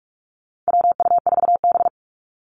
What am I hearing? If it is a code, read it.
Morse code, heard as a keyed tone, about 44 wpm, 700 Hz. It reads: WU4B
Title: Call Signs - 45wpm